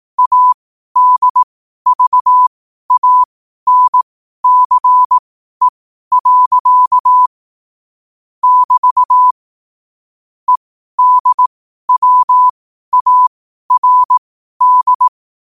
Morse code, heard as a keyed tone, about 18 wpm, spaced as for 12 wpm, 1000 Hz. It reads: ADVANCE. <BT> EDWARD
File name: qotdfiles / 12wpm